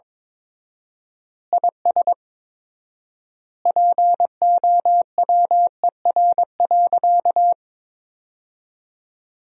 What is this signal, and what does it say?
Morse code, heard as a keyed tone, about 22 wpm, 700 Hz. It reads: IS POWER.
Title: 4x Word-Spacing - Sentences from Top 500 Words - 22wpm